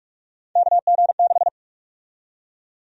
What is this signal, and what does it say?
Morse code, heard as a keyed tone, about 43 wpm, 700 Hz. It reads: KG6
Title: USA Call Sign Prefixes - Repeated 2x - 45wpm